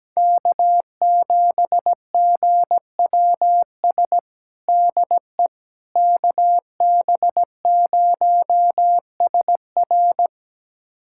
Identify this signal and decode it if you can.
Morse code, heard as a keyed tone, about 17 wpm, 700 Hz. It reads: K7GWS DE KB0SR
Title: USA Callsign DE Callsign - 5 Characters - 17wpm